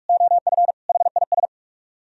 Morse code, heard as a keyed tone, about 57 wpm, 700 Hz. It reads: OF HIS